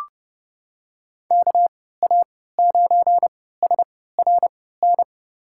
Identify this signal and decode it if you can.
Morse code, heard as a keyed tone, about 31 wpm, spaced as for 17 wpm, 700 Hz. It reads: KA9SRN
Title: USA Call Signs - 6 Characters - 3x Character Spacing - 30wpm